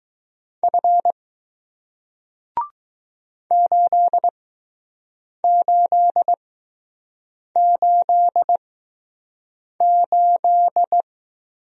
Morse code, heard as a keyed tone, about 20 wpm, spaced as for 6 wpm, 700 Hz. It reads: F 8888